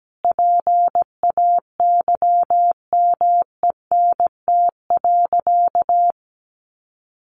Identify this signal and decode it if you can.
Morse code, heard as a keyed tone, about 17 wpm, 700 Hz. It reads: PAYMENT.
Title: Sentences from Top 1200 Words 17wpm